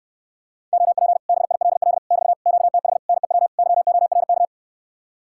Morse code, heard as a keyed tone, about 69 wpm, 700 Hz. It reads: OP BILL <BT> QTH NEW YORK